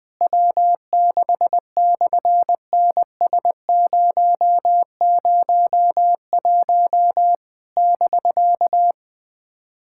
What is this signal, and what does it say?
Morse code, heard as a keyed tone, about 18 wpm, 700 Hz. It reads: W6/NS001 <BK>
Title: SOTA QSO Practice 20wpm